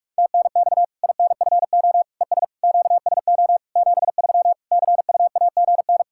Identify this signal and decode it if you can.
Morse code, heard as a keyed tone, about 41 wpm, 700 Hz. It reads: TNX INFO ES QSO 73 CUAGN
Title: K1USN SST QSO Practice 45wpm